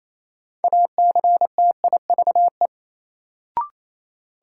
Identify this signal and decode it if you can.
Morse code, heard as a keyed tone, about 28 wpm, 700 Hz. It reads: ACTIVE
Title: Sentences from Top 900 Words - New Word Drill - 28wpm